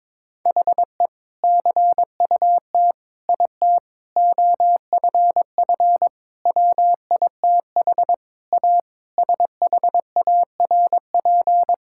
Morse code, heard as a keyed tone, about 22 wpm, 700 Hz. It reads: HE CUT IT OFF WITH A SHARP
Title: Sentences from Top 1600 Words 22wpm